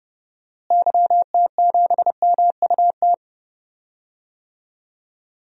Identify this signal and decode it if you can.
Morse code, heard as a keyed tone, about 30 wpm, 700 Hz. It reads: YT7MUT
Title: DXCC-100 Call Signs - 6 Characters - Repeated 3x - 30wpm